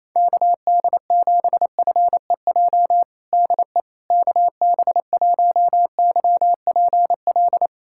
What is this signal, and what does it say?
Morse code, heard as a keyed tone, about 28 wpm, 700 Hz. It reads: KD7FEJ DE KB1YPL
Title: USA Callsign DE Callsign - 6 Characters - 28wpm